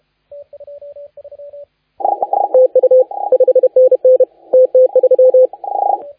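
Morse code, heard as a keyed tone, about 31 wpm, 510 Hz, with radio noise. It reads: TU 5NN TT3
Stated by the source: Morse contest audio recorded by LA4C during the SAC CW 2018 contest.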